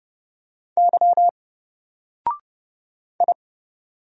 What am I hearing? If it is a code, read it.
Morse code, heard as a keyed tone, about 29 wpm, 700 Hz. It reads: Y I
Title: Single Letter-Number 30wpm